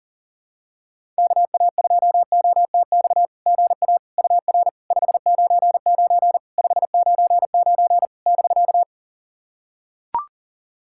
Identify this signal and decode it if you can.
Morse code, heard as a keyed tone, about 40 wpm, 690 Hz. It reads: KA2OTX GA UR 599 599 <BK>
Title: SOTA QSO Practice 40wpm